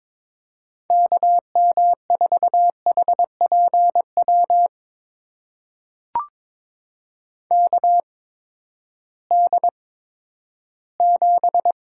Morse code, heard as a keyed tone, about 22 wpm, spaced as for 12 wpm, 700 Hz. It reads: KM4HPW KD7
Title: USA Call Signs - 6 Characters - 8x Character Spacing - 22wpm